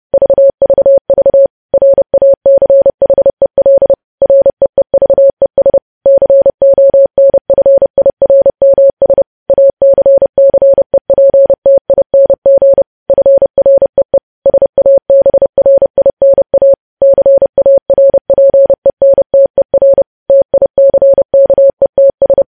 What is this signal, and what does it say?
Morse code, heard as a keyed tone, about 30 wpm, 550 Hz. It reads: VVV RACHEL REEVES CONFIRMS ACCEPTING FREE SABRINA CARPENTER TICKETS
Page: news headlines in morse code at 30 wpm